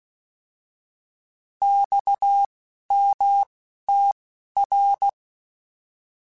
Morse code, heard as a keyed tone, about 16 wpm, spaced as for 11 wpm, 800 Hz. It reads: XMTR
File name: Abreviations / Abreviations mp3s / Abreviations@10